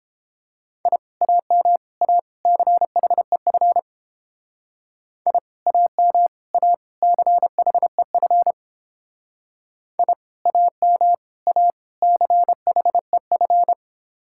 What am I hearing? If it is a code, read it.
Morse code, heard as a keyed tone, about 28 wpm, 700 Hz. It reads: I AM A CHEF I AM A CHEF I AM A CHEF